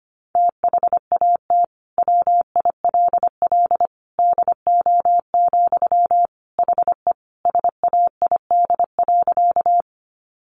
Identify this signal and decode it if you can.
Morse code, heard as a keyed tone, about 25 wpm, 700 Hz. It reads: THAT WILL DO, HE SAID.
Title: Sentences from Top 100 Words 25wpm